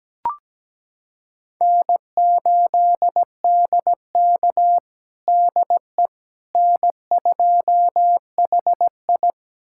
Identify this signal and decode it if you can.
Morse code, heard as a keyed tone, about 17 wpm, 700 Hz. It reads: N8DK DE N2HI
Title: USA Callsign DE Callsign - 4 Characters - 17wpm